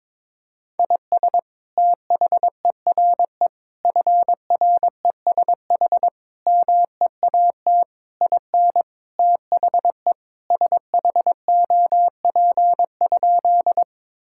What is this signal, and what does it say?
Morse code, heard as a keyed tone, about 22 wpm, 700 Hz. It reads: IS THERE FRESH MEAT IN THE SHOP?
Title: Sentences from Top 1600 Words 22wpm